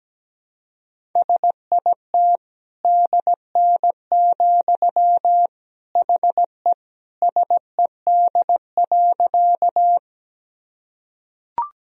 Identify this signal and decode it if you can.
Morse code, heard as a keyed tone, about 17 wpm, 700 Hz. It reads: SIT DN, HE SED.
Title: Sentences from Top 500 Words and Abbreviations 17wpm